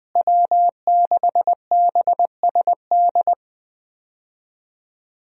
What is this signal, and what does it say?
Morse code, heard as a keyed tone, about 20 wpm, 700 Hz. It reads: W6BSD
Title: USA Call Signs - 5 Characters - Repeated 3x - 20wpm